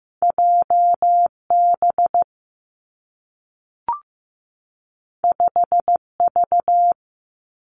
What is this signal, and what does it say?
Morse code, heard as a keyed tone, about 15 wpm, 700 Hz. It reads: JB 5V
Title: Two Challenging Letter-Number Combinations 15wpm